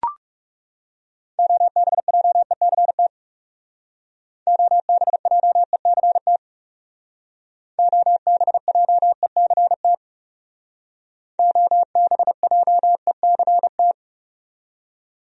Morse code, heard as a keyed tone, about 40 wpm, 700 Hz. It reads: OBJECT OBJECT OBJECT OBJECT